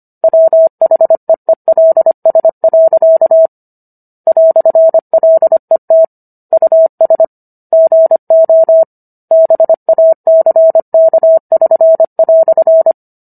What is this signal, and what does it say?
Morse code, heard as a keyed tone, about 25 wpm, 650 Hz. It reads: WHEELS. "LET US GO BACK<SN>"